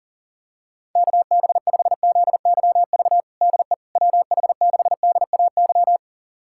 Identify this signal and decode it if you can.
Morse code, heard as a keyed tone, about 37 wpm, 700 Hz. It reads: KB5ZYV DE WH6DAY
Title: USA Callsign DE Callsign - 6 Characters - 40wpm